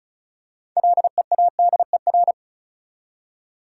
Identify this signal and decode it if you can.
Morse code, heard as a keyed tone, about 36 wpm, 700 Hz. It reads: LEADER
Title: Six Letter Words - 35wpm